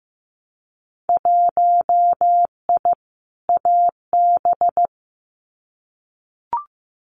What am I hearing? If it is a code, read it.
Morse code, heard as a keyed tone, about 15 wpm, 700 Hz. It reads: 1I AB
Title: Winter Field Day Exchanges Encore 15wpm